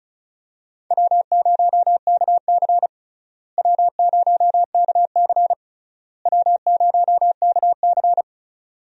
Morse code, heard as a keyed tone, about 36 wpm, 700 Hz. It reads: W0KC W0KC W0KC
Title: USA Call Signs - 4 Characters - Repeated 3x - 35wpm